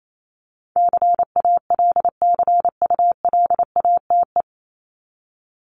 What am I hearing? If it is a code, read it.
Morse code, heard as a keyed tone, about 28 wpm, 700 Hz. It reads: CALCULATE
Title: Top 1751-2000 Words 28wpm